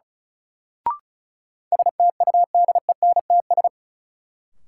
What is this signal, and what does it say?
Morse code, heard as a keyed tone, about 36 wpm, 700 Hz. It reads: STUDENTS
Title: Sets of 5 Words Encore - New Word Drill - 35wpm